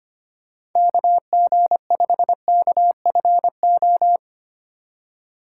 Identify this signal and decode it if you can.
Morse code, heard as a keyed tone, about 25 wpm, 700 Hz. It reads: KG5KFO
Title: USA Call Signs - 6 Characters - 2x Character Spacing - 25wpm